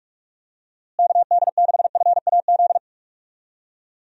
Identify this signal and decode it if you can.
Morse code, heard as a keyed tone, about 46 wpm, 700 Hz. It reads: KD6FAZ
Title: USA Call Signs - 6 Characters - Repeated 2x - 45wpm